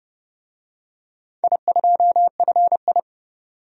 Morse code, heard as a keyed tone, about 29 wpm, 700 Hz. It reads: I2FI
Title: DXCC-100 Call Signs - 4 Characters - 30wpm